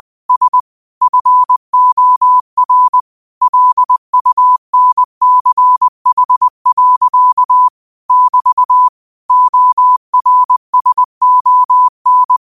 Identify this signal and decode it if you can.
Morse code, heard as a keyed tone, about 20 wpm, 1000 Hz. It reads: S FOR LUNCH. <BT> ORSON